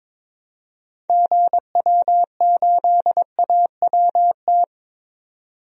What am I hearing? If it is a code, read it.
Morse code, heard as a keyed tone, about 22 wpm, 700 Hz. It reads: GW8AWT
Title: DXCC-100 Call Signs - 6 Characters - 22wpm